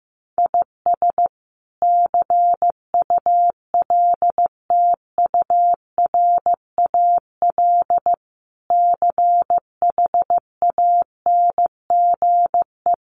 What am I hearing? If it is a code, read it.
Morse code, heard as a keyed tone, about 14 wpm, 700 Hz. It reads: IS CULTURAL CHANGE
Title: Sentences from Top 1500 Words 15wpm